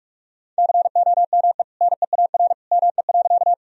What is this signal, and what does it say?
Morse code, heard as a keyed tone, about 46 wpm, 700 Hz. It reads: COME NEAR ME.
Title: Sentences from Top 1000 Words - Review Part 1 - 45wpm